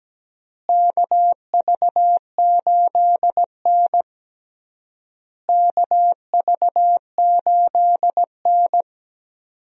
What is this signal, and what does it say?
Morse code, heard as a keyed tone, about 17 wpm, 700 Hz. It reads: KV8N KV8N
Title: USA Call Signs - 4 Characters - Repeated 3x - 17wpm